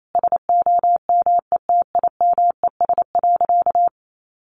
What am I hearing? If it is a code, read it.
Morse code, heard as a keyed tone, about 28 wpm, 700 Hz. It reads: SOMETIMES.
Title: Sentences from Top 600 Words 28wpm